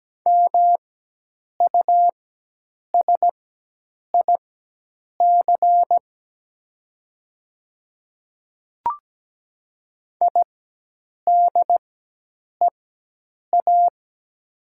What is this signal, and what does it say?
Morse code, heard as a keyed tone, about 17 wpm, spaced as for 8 wpm, 700 Hz. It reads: MUSIC IDEA
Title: 4x Character Spacing - Top 500 Words - 17wpm